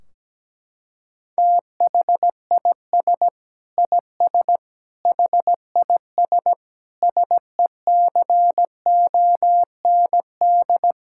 Morse code, heard as a keyed tone, about 17 wpm, 700 Hz. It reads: THIS IS HIS SECOND
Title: Sentences from Top 1600 Words 17wpm